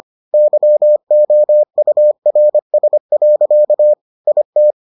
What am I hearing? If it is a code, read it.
Morse code, heard as a keyed tone, about 25 wpm, 600 Hz. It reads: YOURS. IT